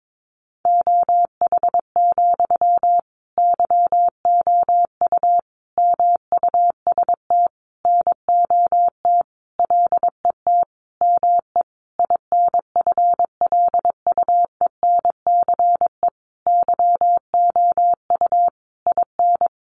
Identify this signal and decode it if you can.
Morse code, heard as a keyed tone, about 22 wpm, 700 Hz. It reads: OH, YOU MUST NOT LET ME INFLUENCE YOU IN
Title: Sentences from Top 1100 Words 22wpm